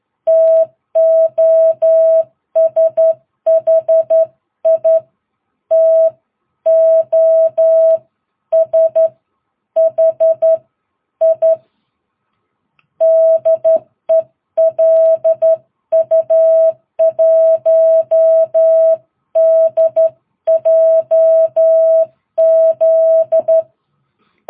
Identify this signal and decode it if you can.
Morse code, heard as a keyed tone, about 12 wpm, spaced as for 10 wpm, 660 Hz. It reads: TOSHI TOSHI DE LU1DJZ
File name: jh8cbh / 2022cw